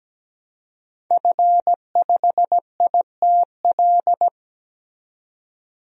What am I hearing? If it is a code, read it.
Morse code, heard as a keyed tone, about 17 wpm, 700 Hz. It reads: F5ITL
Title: DXCC-100 Call Signs - 5 Characters - 17wpm